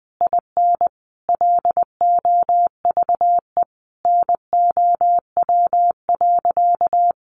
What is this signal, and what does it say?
Morse code, heard as a keyed tone, about 20 wpm, 700 Hz. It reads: IN LOVE NOW.
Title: Sentences from Top 1300 Words 20wpm